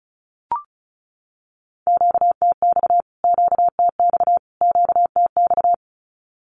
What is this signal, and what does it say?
Morse code, heard as a keyed tone, about 34 wpm, 700 Hz. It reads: QTX QTX QTX
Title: Q-Codes Repeated 3x 35wpm